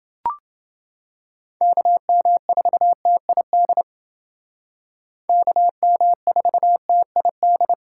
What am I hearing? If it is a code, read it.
Morse code, heard as a keyed tone, about 29 wpm, 700 Hz. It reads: KM4TID KM4TID